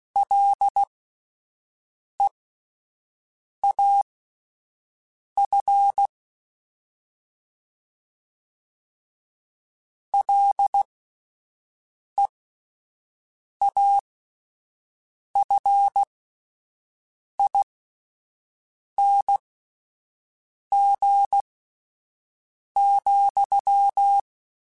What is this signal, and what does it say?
Morse code, heard as a keyed tone, about 16 wpm, spaced as for 5 wpm, 800 Hz. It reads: LEAF LEAFING,